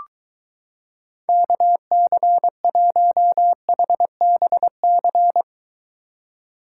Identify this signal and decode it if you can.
Morse code, heard as a keyed tone, about 23 wpm, 700 Hz. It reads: KC1HBC